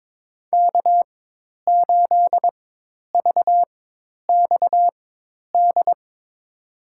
Morse code, heard as a keyed tone, about 22 wpm, spaced as for 10 wpm, 700 Hz. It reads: K8VXD
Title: USA Call Signs - 5 Characters - 4x Character Spacing - 22wpm